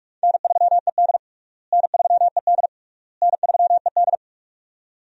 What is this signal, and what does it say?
Morse code, heard as a keyed tone, about 41 wpm, 700 Hz. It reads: N3ED N3ED N3ED
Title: USA Call Signs - 4 Characters - Repeated 3x - 45wpm